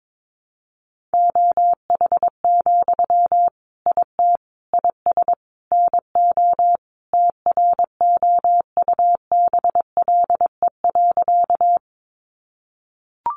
Morse code, heard as a keyed tone, about 22 wpm, 700 Hz. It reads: OH, IT IS NO TROUBLE.
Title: Sentences from Top 1200 Words 22wpm